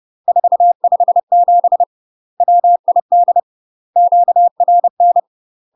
Morse code, heard as a keyed tone, about 30 wpm, 700 Hz. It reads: 457 WID QRN
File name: UK MASTER QSOs TRAINING/30WPM